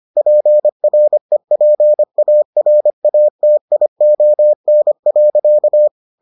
Morse code, heard as a keyed tone, about 25 wpm, 600 Hz. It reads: PREPARATION.